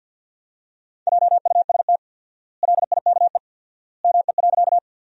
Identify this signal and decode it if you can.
Morse code, heard as a keyed tone, about 48 wpm, 700 Hz. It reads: JUST LIKE ME.
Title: 4x Word-Spacing - Sentences from Top 200 Words - 50wpm